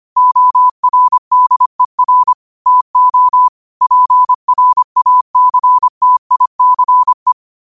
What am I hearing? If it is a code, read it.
Morse code, heard as a keyed tone, about 25 wpm, 1000 Hz. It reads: ORDER TO PRACTICE
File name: qotdfiles / 25wpm